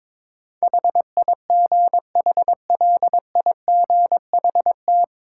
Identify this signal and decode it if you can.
Morse code, heard as a keyed tone, about 22 wpm, 700 Hz. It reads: HIGHLIGHT
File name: 9 Letter Words - First 3 Letters Given - 22wpm